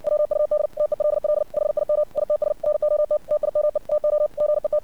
Morse code, heard as a keyed tone, about 60 wpm, 600 Hz. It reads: WUD TELL HIM ITS NOT TIME TO GET
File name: archived / CRSnet